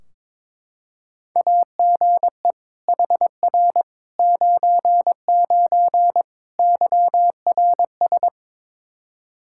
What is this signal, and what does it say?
Morse code, heard as a keyed tone, about 22 wpm, 700 Hz. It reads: AGE HR 99 YRS
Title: Practice QSO - Exchange 3 - More Info - 22wpm